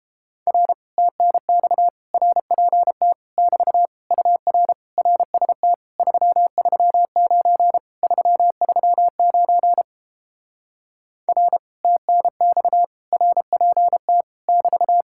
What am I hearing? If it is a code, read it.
Morse code, heard as a keyed tone, about 30 wpm, 700 Hz. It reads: R TNX RPT <BT> UR RST 339 339 R TNX RPT <BT>